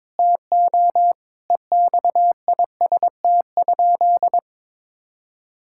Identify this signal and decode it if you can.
Morse code, heard as a keyed tone, about 22 wpm, 700 Hz. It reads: TO EXIST?
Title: Sentences from Top 1000 Words - Review Part 5 - 22wpm